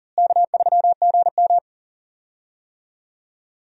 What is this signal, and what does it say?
Morse code, heard as a keyed tone, about 39 wpm, 700 Hz. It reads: K3GM
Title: USA Call Signs - 4 Characters - Repeated 2x - 40wpm